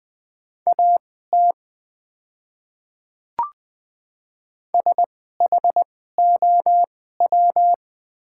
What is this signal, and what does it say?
Morse code, heard as a keyed tone, about 21 wpm, spaced as for 15 wpm, 700 Hz. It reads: AT SHOW
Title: 2x Character Spacing - Top 500 Words - 20wpm